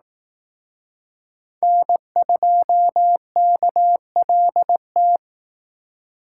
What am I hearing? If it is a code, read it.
Morse code, heard as a keyed tone, about 18 wpm, 700 Hz. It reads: N2KLT
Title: Speed Racing - USA Call Signs 15wpm